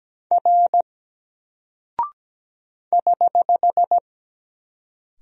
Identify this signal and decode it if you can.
Morse code, heard as a keyed tone, about 17 wpm, 700 Hz. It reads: R <HH>
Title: Single Character (Letter:Number:Punctuation:Prosign) 17wpm